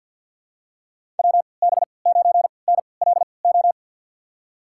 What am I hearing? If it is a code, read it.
Morse code, heard as a keyed tone, about 53 wpm, spaced as for 29 wpm, 700 Hz. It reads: WB9NLO
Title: USA Call Signs - 6 Characters - 3x Character Spacing - 50wpm